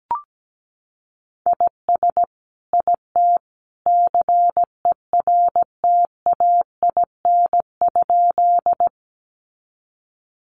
Morse code, heard as a keyed tone, about 17 wpm, 700 Hz. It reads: IS IT CERTAIN?
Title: Sentences from Top 500 Words 17wpm